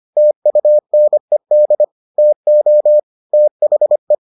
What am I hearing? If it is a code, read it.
Morse code, heard as a keyed tone, about 25 wpm, 600 Hz. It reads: TUNED TO THE